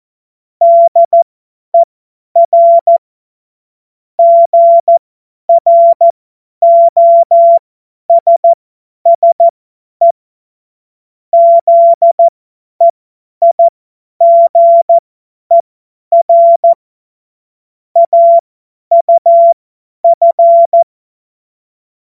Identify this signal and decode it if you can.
Morse code, heard as a keyed tone, about 14 wpm, spaced as for 10 wpm, 680 Hz. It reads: DER GROSSE ZEIGER AUF
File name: Witz-28-Oktober-2025-50-F70